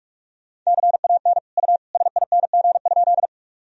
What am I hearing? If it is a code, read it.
Morse code, heard as a keyed tone, about 47 wpm, 700 Hz. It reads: CAN U SING?
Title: Sentences from Top 400 Words and Abbreviations 45wpm